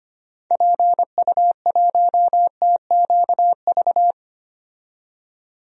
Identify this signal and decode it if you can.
Morse code, heard as a keyed tone, about 25 wpm, 700 Hz. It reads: PU1TQV
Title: DXCC-100 Call Signs - 6 Characters - 25wpm